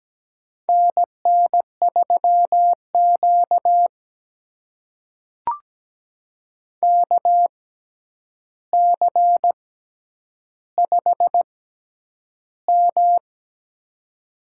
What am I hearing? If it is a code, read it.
Morse code, heard as a keyed tone, about 17 wpm, spaced as for 9 wpm, 700 Hz. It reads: NN3Q KC5M
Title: USA Call Signs - 4 Characters - 6x Character Spacing - 17wpm